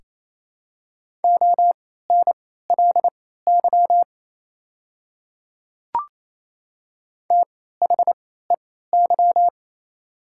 Morse code, heard as a keyed tone, about 28 wpm, spaced as for 16 wpm, 690 Hz. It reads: ONLY THEY
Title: 3x Character Spacing - Top 500 Words - 28wpm